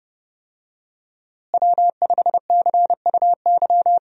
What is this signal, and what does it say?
Morse code, heard as a keyed tone, about 30 wpm, 700 Hz. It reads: W5CUY
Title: USA Call Signs - 5 Characters - Repeated 2x - 30wpm